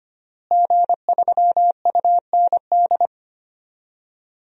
Morse code, heard as a keyed tone, about 25 wpm, 700 Hz. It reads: G3UND
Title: DXCC-100 Call Signs - 5 Characters - Repeated 3x - 25wpm